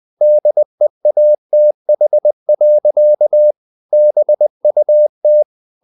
Morse code, heard as a keyed tone, about 20 wpm, 600 Hz. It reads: DEATH. BUT
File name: the old man and the sea part 2 20wpm